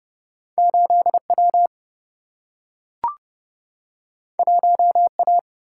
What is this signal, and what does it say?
Morse code, heard as a keyed tone, about 29 wpm, 700 Hz. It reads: 8 W 1 A
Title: Single-Letter Single-Number 30wpm